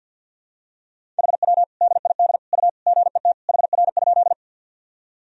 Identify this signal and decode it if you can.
Morse code, heard as a keyed tone, about 52 wpm, 700 Hz. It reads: HW DID U GET HR?
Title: Sentences from Top 200 Words and Abbreviations 50wpm